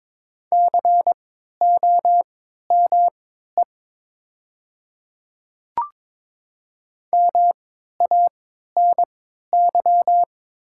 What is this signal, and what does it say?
Morse code, heard as a keyed tone, about 22 wpm, spaced as for 12 wpm, 700 Hz. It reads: COME MANY
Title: 3x Character Spacing - Top 100 Words - 22wpm